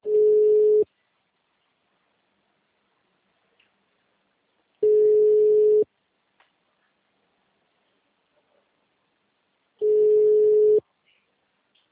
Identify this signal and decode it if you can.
Ringback tone (Europe, 425 Hz).